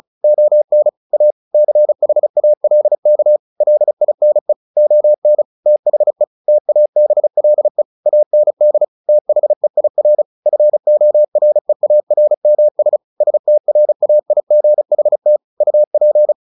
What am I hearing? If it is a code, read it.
Morse code, heard as a keyed tone, about 34 wpm, 600 Hz. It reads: ON A CHALK LINE ON THE TABLE AND THEIR FOREARMS STRAIGHT UP
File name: the old man and the sea / the old man and the sea 1 35wpm